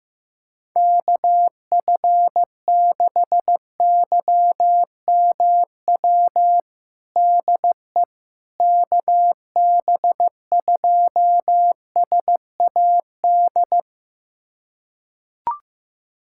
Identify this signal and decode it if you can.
Morse code, heard as a keyed tone, about 15 wpm, 700 Hz. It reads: KF6YMW DE KB2SAD
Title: USA Callsign DE Callsign 15wpm